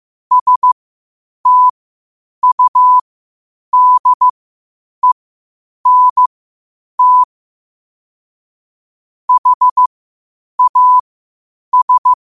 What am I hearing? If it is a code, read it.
Morse code, heard as a keyed tone, about 15 wpm, spaced as for 8 wpm, 1000 Hz. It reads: STUDENT HAS
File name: qotdfiles / 8wpm